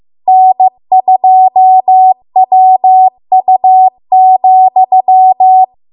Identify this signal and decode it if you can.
Morse code, heard as a keyed tone, about 15 wpm, 750 Hz. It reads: N2WU,